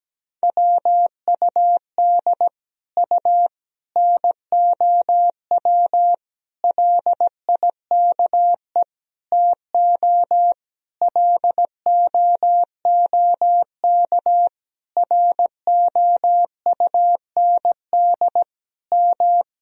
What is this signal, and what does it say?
Morse code, heard as a keyed tone, about 17 wpm, 700 Hz. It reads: WUD U NOW LIKE TO LOOK ROUND M
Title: Sentences from Top 200 Words and Abbreviations 17wpm